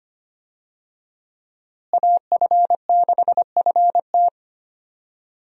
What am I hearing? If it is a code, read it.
Morse code, heard as a keyed tone, about 25 wpm, 700 Hz. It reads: AF6FT
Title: USA Call Signs - 5 Characters - Repeated 3x - 25wpm